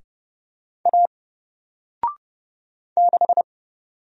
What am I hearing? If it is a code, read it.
Morse code, heard as a keyed tone, about 30 wpm, 700 Hz. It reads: A 6
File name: Single-Letter-Number-30wpm